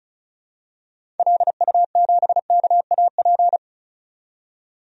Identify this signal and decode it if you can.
Morse code, heard as a keyed tone, about 34 wpm, 700 Hz. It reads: LU7KAP
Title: DXCC-100 Call Signs - 6 Characters - 35wpm